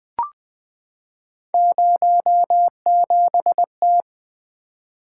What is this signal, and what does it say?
Morse code, heard as a keyed tone, about 20 wpm, 700 Hz. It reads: 07T
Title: Three Letter-Number Combinations - 20wpm